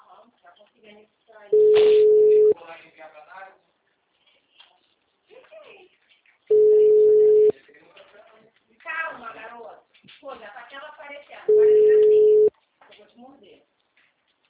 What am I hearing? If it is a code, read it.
Ringback tone (Europe, 425 Hz), with speech.
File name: out- -2016- - - .10364